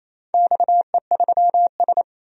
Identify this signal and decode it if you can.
Morse code, heard as a keyed tone, about 28 wpm, 700 Hz. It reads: XE3S
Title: DXCC-100 Call Signs - 4 Characters - 28wpm